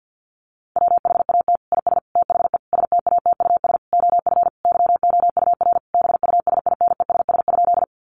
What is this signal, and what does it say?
Morse code, heard as a keyed tone, about 50 wpm, 700 Hz. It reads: WHAT IS THE STATUS OF YOUR BUSINESS?
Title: Sentences from Top 1400 Words 50wpm